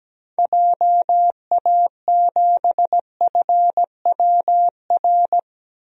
Morse code, heard as a keyed tone, about 17 wpm, 700 Hz. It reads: JA7FWR
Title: DXCC-100 Call Signs - 6 Characters - 17wpm